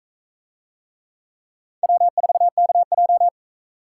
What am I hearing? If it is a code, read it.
Morse code, heard as a keyed tone, about 44 wpm, 700 Hz. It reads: W4KJ